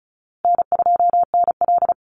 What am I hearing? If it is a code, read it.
Morse code, heard as a keyed tone, about 34 wpm, 700 Hz. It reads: N2NL
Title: Call Signs Repeated 2x - 35wpm